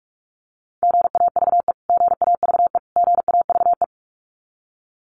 Morse code, heard as a keyed tone, about 44 wpm, 700 Hz. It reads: GAVE GAVE GAVE
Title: Top 500 Words Repeated 45wpm